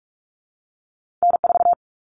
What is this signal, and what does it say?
Morse code, heard as a keyed tone, about 46 wpm, 700 Hz. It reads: N4
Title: Common Cut Numbers - 2-Digit 45wpm